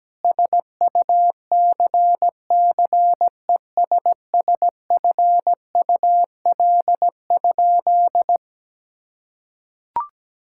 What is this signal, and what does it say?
Morse code, heard as a keyed tone, about 17 wpm, 700 Hz. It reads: SUCCESSFUL?
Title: Sentences from Top 1000 Words - Review Part 5 - 17wpm